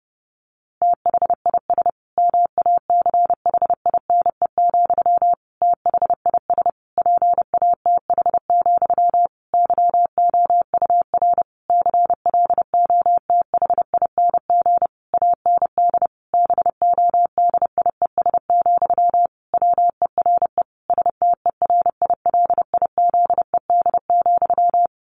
Morse code, heard as a keyed tone, about 30 wpm, 700 Hz. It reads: THIS MACHINE, THIS PATH, YOUR CLOTHING AND BODIES, WERE STERILIZED,
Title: A Sound Of Thunder 30wpm